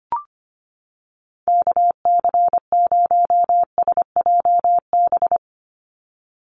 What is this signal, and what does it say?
Morse code, heard as a keyed tone, about 25 wpm, 700 Hz. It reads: KC0SJB
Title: USA Call Signs 25wpm